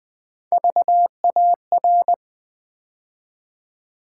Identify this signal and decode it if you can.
Morse code, heard as a keyed tone, about 20 wpm, 700 Hz. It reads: VAR
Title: Build A Word - 3 Letters At A Time - 6-Letter Words - 20wpm